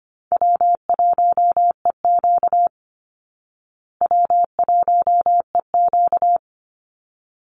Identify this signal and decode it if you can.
Morse code, heard as a keyed tone, about 25 wpm, 700 Hz. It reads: W1EQ W1EQ
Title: Call Signs Repeated 3x - 25wpm